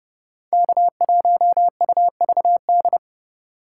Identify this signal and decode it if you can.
Morse code, heard as a keyed tone, about 39 wpm, 700 Hz. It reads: K1UVD